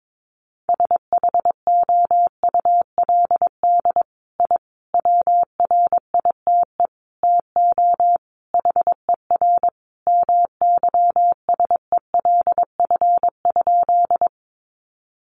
Morse code, heard as a keyed tone, about 22 wpm, 700 Hz. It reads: SHOULD I WRITE TO HER MYSELF?
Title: Sentences from Top 1300 Words 22wpm